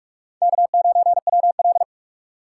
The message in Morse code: K9WL

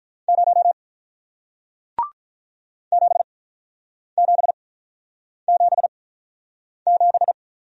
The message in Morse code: 0 7777

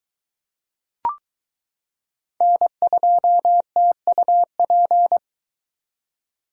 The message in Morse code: N2TUP

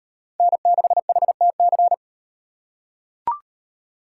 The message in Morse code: N6HTC

